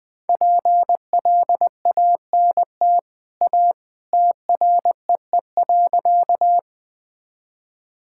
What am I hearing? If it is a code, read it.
Morse code, heard as a keyed tone, about 20 wpm, 700 Hz. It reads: PLANT A TREE.
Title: Easy Sentences with Hints - 20wpm